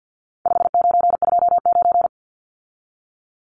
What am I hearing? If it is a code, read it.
Morse code, heard as a keyed tone, about 50 wpm, 700 Hz. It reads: 5929
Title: Numbers - 4-Digit 50wpm